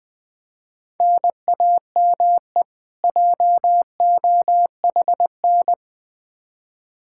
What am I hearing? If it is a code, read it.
Morse code, heard as a keyed tone, about 21 wpm, 700 Hz. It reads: NAME JOHN
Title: Practice QSO - Exchange 1 - Beginning - 20wpm